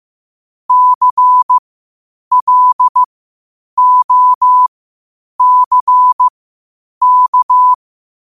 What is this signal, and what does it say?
Morse code, heard as a keyed tone, about 15 wpm, spaced as for 8 wpm, 1000 Hz. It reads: CLOCK
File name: qotdfiles / 8wpm